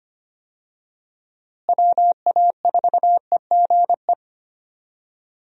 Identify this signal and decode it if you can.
Morse code, heard as a keyed tone, about 25 wpm, 700 Hz. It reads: WA4EGE
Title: USA Call Signs - 6 Characters - Repeated 3x - 25wpm